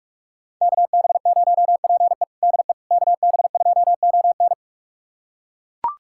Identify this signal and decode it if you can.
Morse code, heard as a keyed tone, about 41 wpm, 700 Hz. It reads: KB0PE DE KB2ON
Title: USA Callsign DE Callsign - 5 Characters - 45wpm